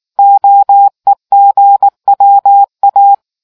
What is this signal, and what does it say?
Morse code, heard as a keyed tone, about 19 wpm, 800 Hz. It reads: OEGWA